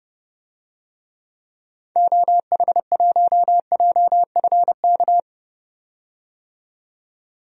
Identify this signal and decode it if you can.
Morse code, heard as a keyed tone, about 32 wpm, 700 Hz. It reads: OH1JFK
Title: DXCC-100 Call Signs - 6 Characters - 30wpm